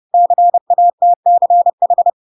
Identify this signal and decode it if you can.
Morse code, heard as a keyed tone, about 30 wpm, 700 Hz. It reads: CATCH
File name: Pinocchio / Pinocchio ch3 30 30 ik1hge